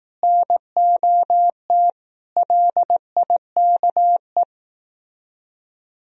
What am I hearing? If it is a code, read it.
Morse code, heard as a keyed tone, about 18 wpm, 700 Hz. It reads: NOT LIKE
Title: Speed-Racing - Sets of 5 Words 15wpm